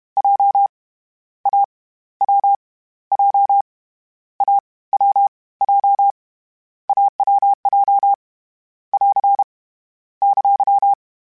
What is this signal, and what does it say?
Morse code, heard as a keyed tone, about 31 wpm, 800 Hz. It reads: J A W J A W J AWJ <AR> !
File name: Difficult Combos / Difficult Combos-mp3s / AWJ @30